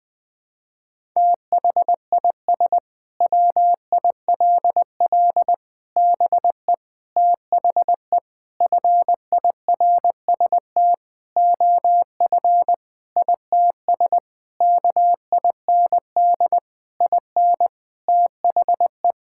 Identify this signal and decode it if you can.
Morse code, heard as a keyed tone, about 20 wpm, 700 Hz. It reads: THIS WILL BE THE FIRST OF ITS KIND IN THE
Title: Sentences with 8-Word Phrases - 20wpm